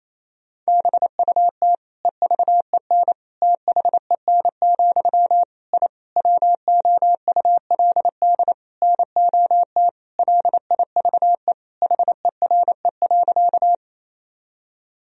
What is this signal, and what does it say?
Morse code, heard as a keyed tone, about 27 wpm, 700 Hz. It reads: BUT EVEN THEN, I WOULD NOT LIVE HERE.
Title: Sentences from Top 1000 Words - Review Part 1 - 28wpm